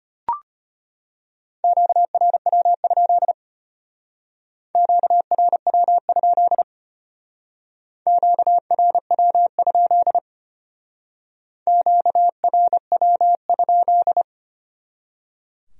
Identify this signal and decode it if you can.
Morse code, heard as a keyed tone, about 33 wpm, 700 Hz. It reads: QRW? QRW? QRW? QRW?